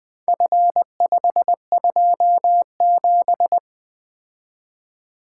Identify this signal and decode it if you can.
Morse code, heard as a keyed tone, about 20 wpm, 700 Hz. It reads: F527
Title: Four Letter-Number Combinations - 20wpm